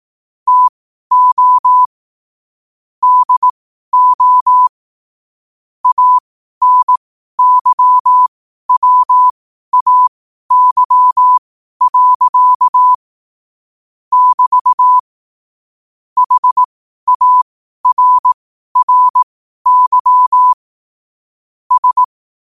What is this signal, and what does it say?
Morse code, heard as a keyed tone, about 18 wpm, spaced as for 12 wpm, 1000 Hz. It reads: TO DO ANYWAY. <BT> HARRY S
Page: 12-wpm-morse-code-quotes